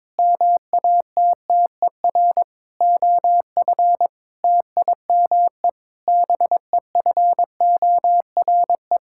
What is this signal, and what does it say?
Morse code, heard as a keyed tone, about 22 wpm, 700 Hz. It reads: MATTER OF TIME BEFORE
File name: practice / 22wpm / Sets of 8 Words - 22wpm